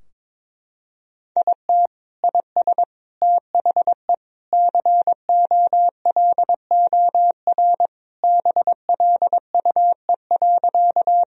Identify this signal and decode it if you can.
Morse code, heard as a keyed tone, about 22 wpm, 700 Hz. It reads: IT IS THE COLOR BLUE.
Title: Sentences from Top 1000 Words - Review Part 3 - 22wpm